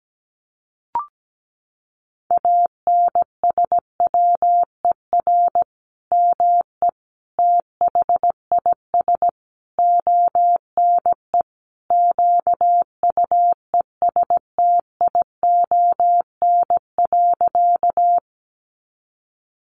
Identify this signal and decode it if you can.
Morse code, heard as a keyed tone, about 17 wpm, 700 Hz. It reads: ANSWER ME THIS ONE QUESTION.
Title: Sentences from Top 400 Words 17wpm